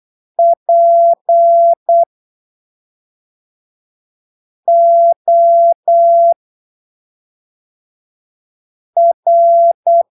Morse code, heard as a keyed tone, about 8 wpm, spaced as for 3 wpm, 680 Hz. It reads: POR